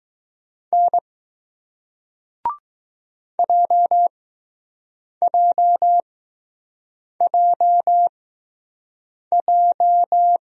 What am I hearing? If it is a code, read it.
Morse code, heard as a keyed tone, about 20 wpm, spaced as for 6 wpm, 700 Hz. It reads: N JJJJ